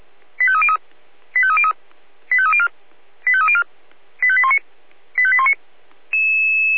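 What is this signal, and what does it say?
Five-tone selective call (ZVEI), heard as quick runs of five tones, tones of about 70 ms. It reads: 85303 85303 85304 85304 85819 85819
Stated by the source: Five-tone ZVEI selective call received with an SDR.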